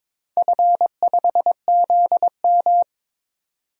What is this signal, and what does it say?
Morse code, heard as a keyed tone, about 22 wpm, 700 Hz. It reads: F5ZM
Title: DXCC-100 Call Signs - 4 Characters - 22wpm